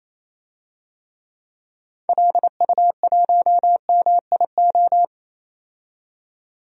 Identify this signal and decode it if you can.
Morse code, heard as a keyed tone, about 28 wpm, 700 Hz. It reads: LU1MIO
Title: DXCC-100 Call Signs - 6 Characters - 28wpm